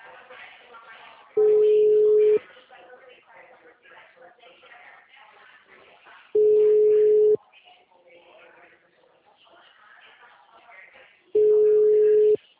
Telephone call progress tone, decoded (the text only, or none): ringback tone